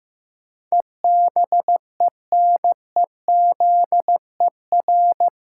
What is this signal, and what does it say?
Morse code, heard as a keyed tone, about 15 wpm, 700 Hz. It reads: EBENEZER
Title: Pangram Challenge v2 - 15wpm